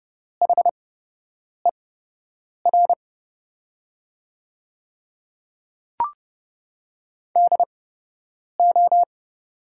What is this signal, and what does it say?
Morse code, heard as a keyed tone, about 30 wpm, spaced as for 8 wpm, 700 Hz. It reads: HER DO
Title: 8x Character Spacing - Top 300 Words - 30wpm